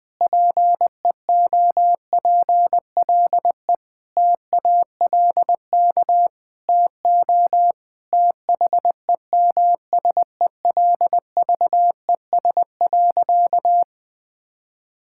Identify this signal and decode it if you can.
Morse code, heard as a keyed tone, about 21 wpm, 700 Hz. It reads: PEOPLE TALK TO THEMSELVES.
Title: Sentences from Top 1000 Words - Review Part 5 - 20wpm